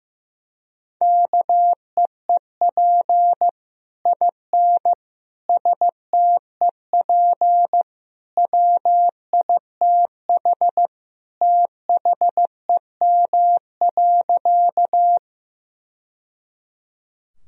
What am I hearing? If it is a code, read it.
Morse code, heard as a keyed tone, about 15 wpm, 700 Hz. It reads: KEEP IN STEP WITH THEM.
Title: Sentences from Top 1000 Words - Review Part 2 - 15wpm